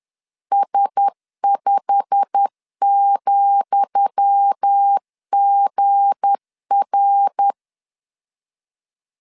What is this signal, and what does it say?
Morse code, heard as a keyed tone, about 10 wpm, 800 Hz. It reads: S5,GR
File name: LEZIONI / 10 SIGNS / 10D COMA 10wpm 3013